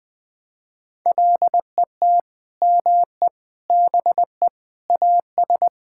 Morse code, heard as a keyed tone, about 20 wpm, 700 Hz. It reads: LET ME BE AS
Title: Sets of 9 Words - 20wpm